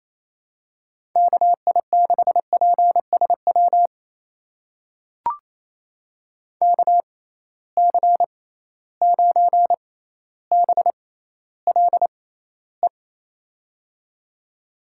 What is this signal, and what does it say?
Morse code, heard as a keyed tone, about 28 wpm, spaced as for 14 wpm, 700 Hz. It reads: KI6PSW KC9BLE
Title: USA Call Signs - 6 Characters - 6x Character Spacing - 28wpm